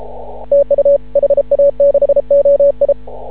Morse code, heard as a keyed tone, about 33 wpm, 560 Hz. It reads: TU HA6OI
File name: CW / ha6oi 2